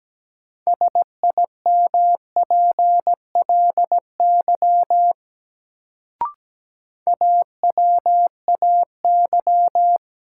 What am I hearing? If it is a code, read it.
Morse code, heard as a keyed tone, about 17 wpm, 700 Hz. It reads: SIMPLY AWAY
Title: Sentences from Top 600 Words - New Word Drill - 17wpm